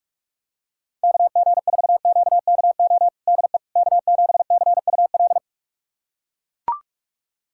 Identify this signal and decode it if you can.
Morse code, heard as a keyed tone, about 42 wpm, 700 Hz. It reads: KG4QKO DE K7CUL
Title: USA Callsign DE Callsign 45wpm